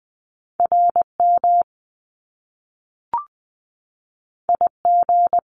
Morse code, heard as a keyed tone, about 20 wpm, 700 Hz. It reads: RM IG
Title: Top Two Letter Combinations in Words 20wpm